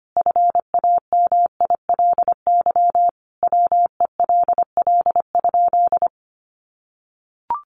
Morse code, heard as a keyed tone, about 25 wpm, 700 Hz. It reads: FAMILY WELL?
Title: Sentences from Top 400 Words 25wpm